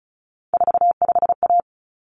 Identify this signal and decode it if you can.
Morse code, heard as a keyed tone, about 34 wpm, 700 Hz. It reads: 45A
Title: Common Cut Numbers - 3-Digit 35wpm